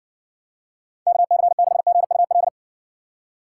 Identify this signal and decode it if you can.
Morse code, heard as a keyed tone, about 60 wpm, 700 Hz. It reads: KC6GUL